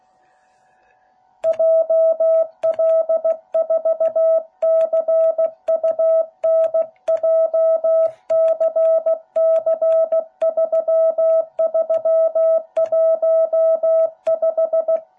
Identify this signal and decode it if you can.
Morse code, heard as a keyed tone, about 16 wpm, 660 Hz. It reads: JL4CUN JCC3315